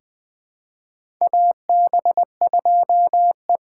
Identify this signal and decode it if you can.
Morse code, heard as a keyed tone, about 20 wpm, 700 Hz. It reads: AB2E
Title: USA Call Signs - 4 Characters - 3x Character Spacing - 20wpm